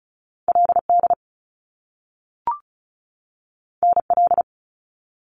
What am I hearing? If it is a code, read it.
Morse code, heard as a keyed tone, about 36 wpm, 700 Hz. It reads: LD NL
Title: Top Two Letter Combinations in Words 35wpm